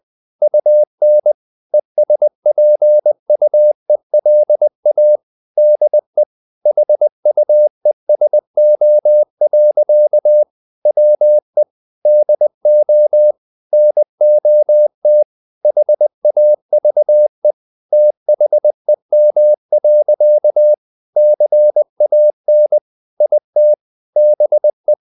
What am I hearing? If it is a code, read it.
Morse code, heard as a keyed tone, about 20 wpm, 600 Hz. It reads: UN ESPUELA DE HUESO. WE DO NOT HAVE THEM. CAN IT BE
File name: the old man and the sea / the old man and the sea part 1 20wpm